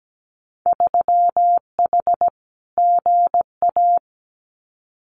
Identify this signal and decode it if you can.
Morse code, heard as a keyed tone, about 17 wpm, 700 Hz. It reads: 3H GA
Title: Winter Field Day Exchanges Encore 17wpm